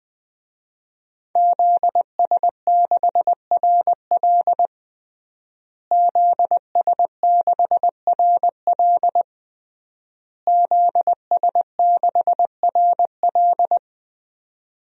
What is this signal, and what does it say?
Morse code, heard as a keyed tone, about 20 wpm, 700 Hz. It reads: ZS6RL ZS6RL ZS6RL
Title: DXCC-100 Call Signs - 5 Characters - Repeated 3x - 20wpm